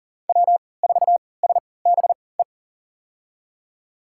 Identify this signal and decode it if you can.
Morse code, heard as a keyed tone, about 37 wpm, spaced as for 22 wpm, 700 Hz. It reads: W4SBE
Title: USA Call Signs - 5 Characters - 3x Character Spacing - 40wpm